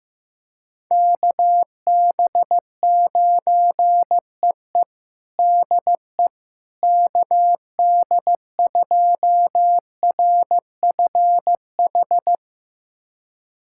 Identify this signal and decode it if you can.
Morse code, heard as a keyed tone, about 15 wpm, 700 Hz. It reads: KB9EE DE KD2RFH
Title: USA Callsign DE Callsign 15wpm